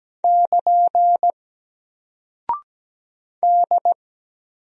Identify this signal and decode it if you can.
Morse code, heard as a keyed tone, about 17 wpm, 700 Hz. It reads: <KN> D
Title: Single Character (Letter:Number:Punctuation:Prosign) 17wpm